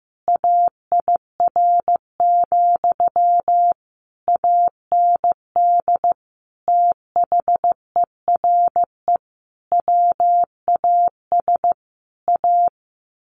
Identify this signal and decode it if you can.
Morse code, heard as a keyed tone, about 15 wpm, 700 Hz. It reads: AIR, AND THERE WAS A
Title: A Sound Of Thunder 15wpm